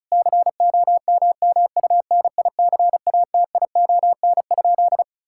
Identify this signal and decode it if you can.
Morse code, heard as a keyed tone, about 36 wpm, 700 Hz. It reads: COMMUNICATION?
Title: Sentences from Top 1000 Words - Review Part 6 - 35wpm